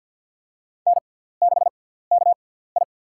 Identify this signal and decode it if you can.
Morse code, heard as a keyed tone, about 53 wpm, spaced as for 17 wpm, 700 Hz. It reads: N6KI
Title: USA Call Signs - 4 Characters - 6x Character Spacing - 50wpm